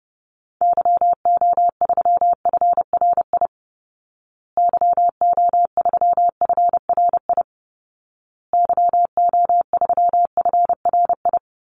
Morse code, heard as a keyed tone, about 28 wpm, 700 Hz. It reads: YO3FRI YO3FRI YO3FRI
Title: Call Signs Repeated 2x - 30wpm